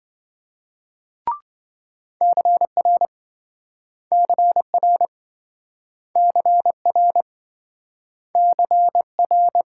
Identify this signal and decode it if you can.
Morse code, heard as a keyed tone, about 27 wpm, 700 Hz. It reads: CR CR CR CR